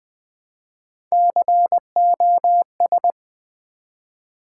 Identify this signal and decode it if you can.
Morse code, heard as a keyed tone, about 20 wpm, 700 Hz. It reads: COS
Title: Build A Word - 1 Letter At A Time - 4-Letter Words - 20wpm